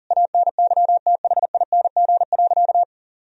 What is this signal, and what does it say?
Morse code, heard as a keyed tone, about 39 wpm, 700 Hz. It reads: ANYTHING.
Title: Sentences from Top 1000 Words - Review Part 3 - 40wpm